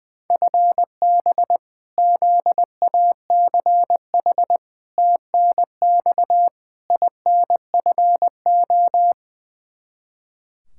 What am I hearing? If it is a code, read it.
Morse code, heard as a keyed tone, about 20 wpm, 700 Hz. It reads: FB ZACH TNX INFO
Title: K1USN SST QSO Practice 20wpm